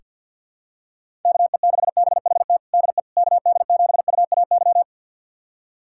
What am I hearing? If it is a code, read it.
Morse code, heard as a keyed tone, about 45 wpm, 700 Hz. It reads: KE6BHT DE KD7UAY
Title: USA Callsign DE Callsign 50wpm